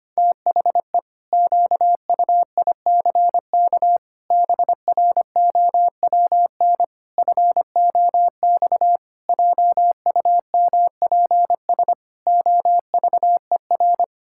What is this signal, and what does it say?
Morse code, heard as a keyed tone, about 25 wpm, 700 Hz. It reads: THE QUICK BROWN FOX JUMPS OVER
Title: Pangram Challenge v2 - 25wpm